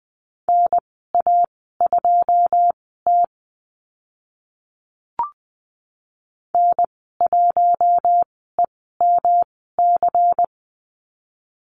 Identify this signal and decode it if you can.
Morse code, heard as a keyed tone, about 20 wpm, spaced as for 15 wpm, 700 Hz. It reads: NA2T N1EMC
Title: 154 Lesson 25g - Letter p - Call Signs 20wpm